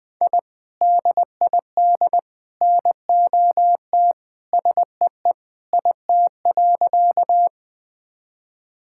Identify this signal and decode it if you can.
Morse code, heard as a keyed tone, about 20 wpm, 700 Hz. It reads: I DID NOT SEE IT.
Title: Sentences with 3-Word Phrases - Spoken Beginning - 20wpm